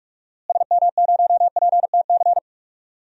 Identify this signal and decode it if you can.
Morse code, heard as a keyed tone, about 46 wpm, 700 Hz. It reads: SM0PTC
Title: DXCC-100 Call Signs - 6 Characters - Repeated 3x - 45wpm